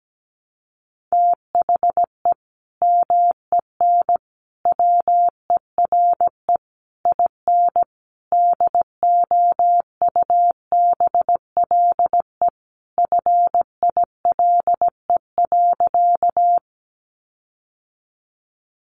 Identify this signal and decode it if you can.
Morse code, heard as a keyed tone, about 17 wpm, 700 Hz. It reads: THE MEN WERE IN DOUBLE FILE.
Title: Sentences from Top 1400 Words 17wpm